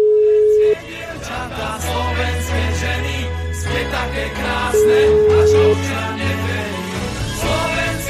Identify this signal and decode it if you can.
Ringback tone (Europe, 425 Hz), with noise.